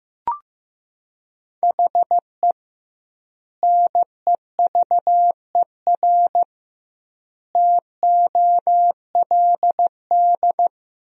Morse code, heard as a keyed tone, about 15 wpm, 700 Hz. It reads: HE NEVER TOLD
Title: 2x Word-Spacing - Sentences from Top 400 Words - 15wpm